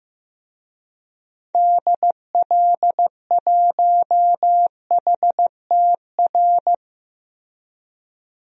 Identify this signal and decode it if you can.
Morse code, heard as a keyed tone, about 16 wpm, 700 Hz. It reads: DL1HTR
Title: DXCC-100 Call Signs - 6 Characters - Repeated 2x - 15wpm